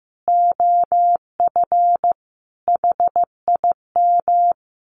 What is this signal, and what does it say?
Morse code, heard as a keyed tone, about 15 wpm, 700 Hz. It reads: OF HIM
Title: Sets of 4 Words 15wpm